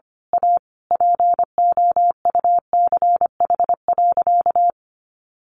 Morse code, heard as a keyed tone, about 25 wpm, 700 Hz. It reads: A POUCH.
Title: Easy Phrases 25wpm